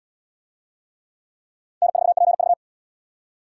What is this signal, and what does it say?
Morse code, heard as a keyed tone, about 65 wpm, 700 Hz. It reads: N4PV